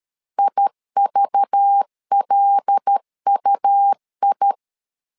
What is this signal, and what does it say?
Morse code, heard as a keyed tone, about 13 wpm, 800 Hz. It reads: IVLUI
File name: LEZIONI / 5 LBVC / 5 LBV 12wpm 1855